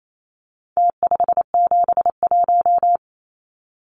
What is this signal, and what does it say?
Morse code, heard as a keyed tone, about 28 wpm, 700 Hz. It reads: T571
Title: Numbers - Contest Style - 4-Digit 28wpm